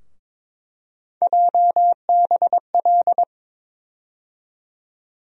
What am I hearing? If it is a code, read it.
Morse code, heard as a keyed tone, about 22 wpm, 700 Hz. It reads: JBL
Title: USA Call Sign Postfixes - 3 Letters 22wpm